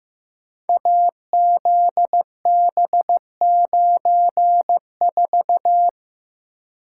Morse code, heard as a keyed tone, about 15 wpm, 700 Hz. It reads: AZB94
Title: Code Groups - 5 Letter:Numbers - 15wpm